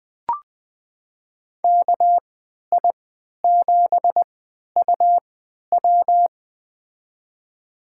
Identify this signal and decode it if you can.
Morse code, heard as a keyed tone, about 21 wpm, spaced as for 12 wpm, 700 Hz. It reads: KI7UW
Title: USA Call Signs - 5 Characters - 3x Character Spacing - 20wpm